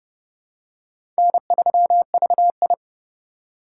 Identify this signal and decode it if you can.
Morse code, heard as a keyed tone, about 29 wpm, 700 Hz. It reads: N3VI
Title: USA Call Signs - 4 Characters - 2x Character Spacing - 30wpm